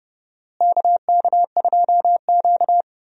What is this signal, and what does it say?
Morse code, heard as a keyed tone, about 29 wpm, 700 Hz. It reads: KK2Q
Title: USA Call Signs - 4 Characters - Repeated 3x - 30wpm